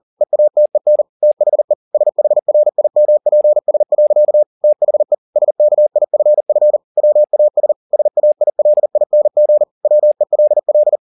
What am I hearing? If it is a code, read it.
Morse code, heard as a keyed tone, about 40 wpm, 600 Hz. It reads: EATEN THE SHRIMPS. THE SKIFF WAS SAILING WELL
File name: the old man and the sea 2 40wpm